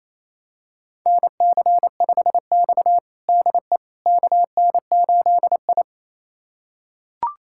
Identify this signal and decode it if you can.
Morse code, heard as a keyed tone, about 27 wpm, 690 Hz. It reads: NC5X DE KN8I
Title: USA Callsign DE Callsign - 4 Characters - 28wpm